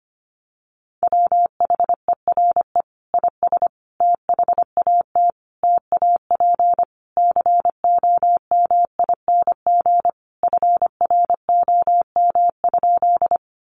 Morse code, heard as a keyed tone, about 25 wpm, 700 Hz. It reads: WHERE IS THAT TAP COMING FROM?
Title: Sentences from Top 1400 Words 25wpm